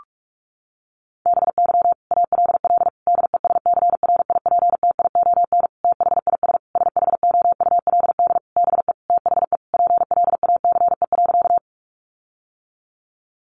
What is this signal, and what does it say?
Morse code, heard as a keyed tone, about 44 wpm, 700 Hz. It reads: BY ALL DESCRIPTION THIS SHOULD BE THE PLACE.
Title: Sentences from Top 1300 Words 45wpm